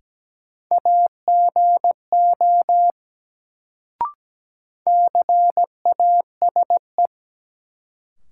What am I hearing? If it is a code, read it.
Morse code, heard as a keyed tone, about 17 wpm, 700 Hz. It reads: AGO CASE
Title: Sets of 2 Words - New Word Drill - 17wpm